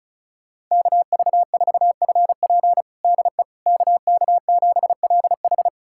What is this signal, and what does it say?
Morse code, heard as a keyed tone, about 36 wpm, 700 Hz. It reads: KV4FP DE KK7LH
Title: USA Callsign DE Callsign - 5 Characters - 35wpm